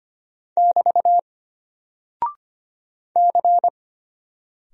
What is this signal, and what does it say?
Morse code, heard as a keyed tone, about 25 wpm, 700 Hz. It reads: <BT> C
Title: Single Character (Letter:Number:Punctuation:Prosign) Alt 25wpm